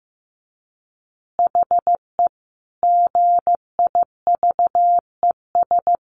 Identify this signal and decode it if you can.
Morse code, heard as a keyed tone, about 15 wpm, 700 Hz. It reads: HE GIVES
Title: Easy Phrases 15wpm